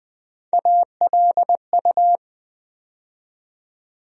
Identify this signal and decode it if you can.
Morse code, heard as a keyed tone, about 21 wpm, 700 Hz. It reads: ALU
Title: USA Call Sign Postfixes - 3 Letters 20wpm